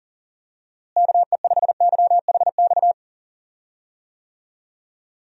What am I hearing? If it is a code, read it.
Morse code, heard as a keyed tone, about 40 wpm, 700 Hz. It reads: KE5YHX
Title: USA Call Signs - 6 Characters - Repeated 3x - 40wpm